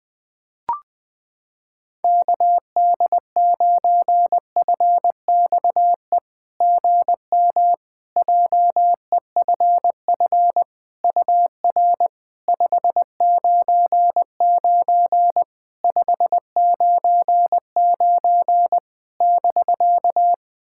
Morse code, heard as a keyed tone, about 20 wpm, 690 Hz. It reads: KD9FXE GM JEFF UR 599 599 <BK>
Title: SOTA QSO Practice 20wpm